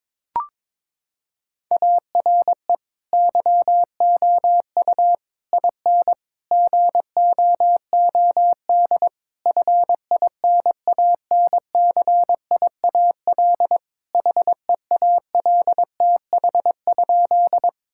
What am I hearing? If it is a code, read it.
Morse code, heard as a keyed tone, about 21 wpm, 700 Hz. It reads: ARE YOU IN GOOD FINANCIAL HEALTH?
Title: Sentences from Top 1000 Words - Review Part 5 - 22wpm